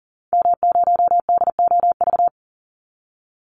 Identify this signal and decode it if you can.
Morse code, heard as a keyed tone, about 40 wpm, 700 Hz. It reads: M0DOV
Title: Call Signs Repeated 2x - 40wpm